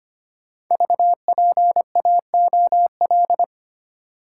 Morse code, heard as a keyed tone, about 25 wpm, 700 Hz. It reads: VPAOL
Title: Code Groups - 5 Letters - 25wpm